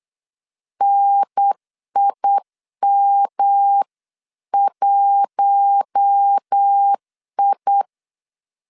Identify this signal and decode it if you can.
Morse code, heard as a keyed tone, about 8 wpm, 800 Hz. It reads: NIM1I
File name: 8wpm